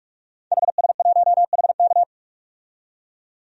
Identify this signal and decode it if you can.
Morse code, heard as a keyed tone, about 46 wpm, 690 Hz. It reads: HS1HK
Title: DXCC-100 Call Signs - 5 Characters - 45wpm